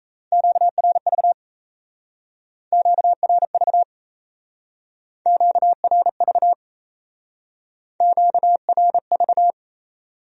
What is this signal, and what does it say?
Morse code, heard as a keyed tone, about 35 wpm, 700 Hz. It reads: QRV QRV QRV QRV